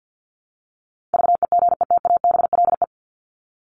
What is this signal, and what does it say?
Morse code, heard as a keyed tone, about 50 wpm, 700 Hz. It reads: VEGETABLE
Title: Top 1251-1500 Words 50wpm